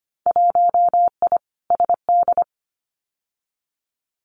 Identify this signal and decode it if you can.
Morse code, heard as a keyed tone, about 25 wpm, 700 Hz. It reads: 1I SD
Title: Winter Field Day Exchanges 25wpm